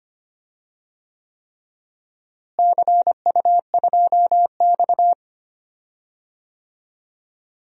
Morse code, heard as a keyed tone, about 25 wpm, 700 Hz. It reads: CU2X
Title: DXCC-100 Call Signs - 4 Characters - Repeated 3x - 25wpm